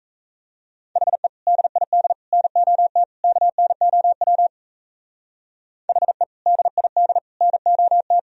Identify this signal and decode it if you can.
Morse code, heard as a keyed tone, about 35 wpm, 700 Hz. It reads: HE DID NOT KNOW HE DID NOT
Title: Speed-Racing - Sets of 4 Words 28wpm